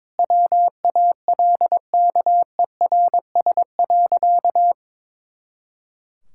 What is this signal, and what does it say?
Morse code, heard as a keyed tone, about 22 wpm, 700 Hz. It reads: WALKERS.
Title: Pangram Challenge v2 - 22wpm